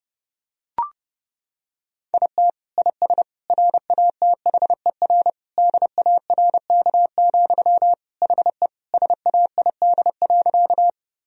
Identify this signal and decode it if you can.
Morse code, heard as a keyed tone, about 31 wpm, 700 Hz. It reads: IT IS RATHER DARK, HE SAID.
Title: Sentences from Top 1000 Words - Review Part 4 - 30wpm